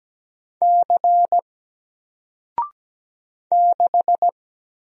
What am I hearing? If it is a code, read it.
Morse code, heard as a keyed tone, about 17 wpm, 700 Hz. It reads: C 6
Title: Single Letter-Number 17wpm